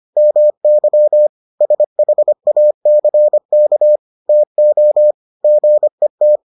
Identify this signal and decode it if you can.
Morse code, heard as a keyed tone, about 25 wpm, 600 Hz. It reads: MY SHACK TO GET